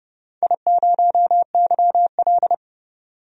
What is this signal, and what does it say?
Morse code, heard as a keyed tone, about 30 wpm, 700 Hz. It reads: I0YL
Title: DXCC-100 Call Signs - 4 Characters - 30wpm